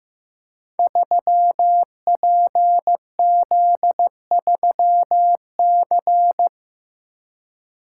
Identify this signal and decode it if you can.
Morse code, heard as a keyed tone, about 15 wpm, 700 Hz. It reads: 3PZ3C
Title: Code Groups - 5 Letter:Numbers - 15wpm